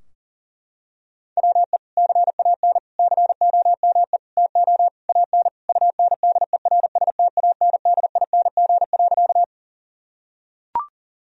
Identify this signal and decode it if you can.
Morse code, heard as a keyed tone, about 40 wpm, 700 Hz. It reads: WE CAN COME TO AN UNDERSTANDING.
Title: Sentences from Top 1000 Words - Review Part 3 - 40wpm